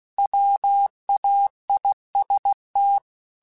Morse code, heard as a keyed tone, about 16 wpm, 800 Hz. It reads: WAIST
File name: Chuckles / Chuckles-mp3s / MiddleAge @15